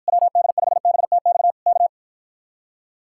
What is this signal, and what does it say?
Morse code, heard as a keyed tone, about 54 wpm, 700 Hz. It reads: WD5BTX K